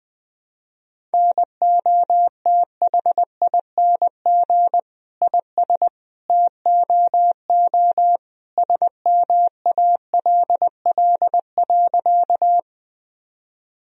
Morse code, heard as a keyed tone, about 20 wpm, 700 Hz. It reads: NOTHING IS TOO SMALL.
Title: Sentences from Top 1000 Words - Review Part 1 - 20wpm